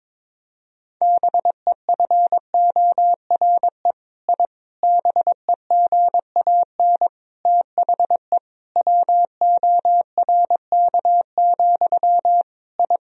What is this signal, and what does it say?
Morse code, heard as a keyed tone, about 22 wpm, 700 Hz. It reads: BEFORE I BEGAN THE WORK, I
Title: Sentences from Top 1000 Words - Review Part 4 - 22wpm